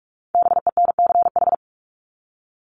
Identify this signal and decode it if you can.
Morse code, heard as a keyed tone, about 44 wpm, 700 Hz. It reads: BENCH
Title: Top 1251-1500 Words 45wpm